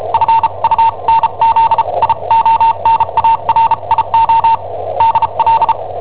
Morse code, heard as a keyed tone, about 34 wpm, spaced as for 29 wpm, 910 Hz, with radio noise. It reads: FUNZIONARIO DL
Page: Cassetta esercizi preparata da I7OHP per IW5ELP